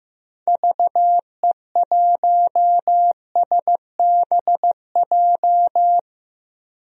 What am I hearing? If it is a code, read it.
Morse code, heard as a keyed tone, about 16 wpm, 700 Hz. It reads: VE1SBJ
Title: DXCC-100 Call Signs - 6 Characters - 15wpm